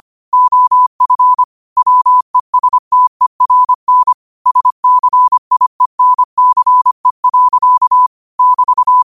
Morse code, heard as a keyed tone, about 25 wpm, 1000 Hz. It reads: OF WESTERN SCIENCE. <BT>
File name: qotdfiles / 25wpm